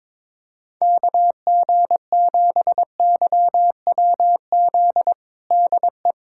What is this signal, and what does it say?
Morse code, heard as a keyed tone, about 22 wpm, 700 Hz. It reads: KG7YWZ DE
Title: USA Callsign DE Callsign - 6 Characters - 22wpm